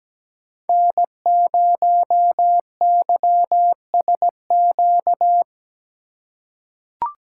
Morse code, heard as a keyed tone, about 17 wpm, 700 Hz. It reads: N0YSQ
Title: K1USN SST QSO Practice - 17wpm